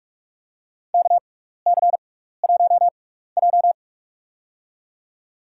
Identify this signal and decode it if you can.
Morse code, heard as a keyed tone, about 47 wpm, spaced as for 16 wpm, 690 Hz. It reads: KC1J
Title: USA Call Signs - 6x Character Spacing - 45wpm